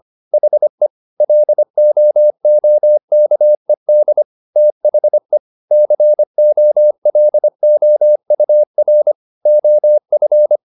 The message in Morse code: HE LOOKED THE COLOUR OF

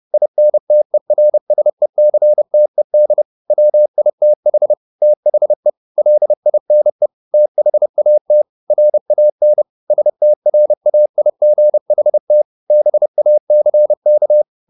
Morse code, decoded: INTERSECTED WITH THE LINE THAT RAN STRAIGHT BACK

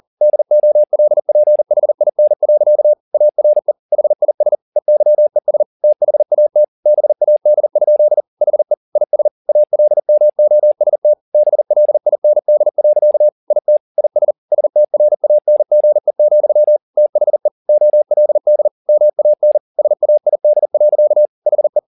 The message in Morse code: DOLPHIN. ARE HIS EYES THAT BAD? HE IS ALMOST BLIND. IT IS STRANGE, THE OLD MAN SAID. HE

